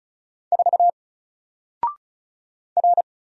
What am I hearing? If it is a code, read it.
Morse code, heard as a keyed tone, about 36 wpm, 700 Hz. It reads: 4 R
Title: Single Letter-Number 35wpm